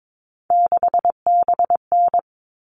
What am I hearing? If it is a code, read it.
Morse code, heard as a keyed tone, about 22 wpm, 700 Hz. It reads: 6BN
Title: All Cut Numbers - 3-Digit 22wpm